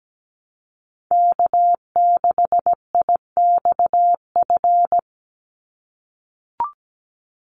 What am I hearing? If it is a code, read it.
Morse code, heard as a keyed tone, about 17 wpm, 700 Hz. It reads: K6IXF
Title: USA Call Signs 17wpm